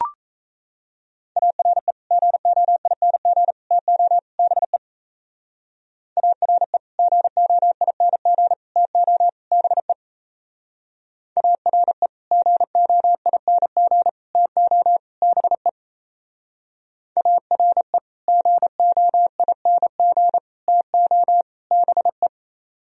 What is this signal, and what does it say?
Morse code, heard as a keyed tone, about 34 wpm, 700 Hz. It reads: ARE GOING TO BE ARE GOING TO BE ARE GOING TO BE ARE GOING TO BE
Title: Speed-Racing - Sets of 4 Words 28wpm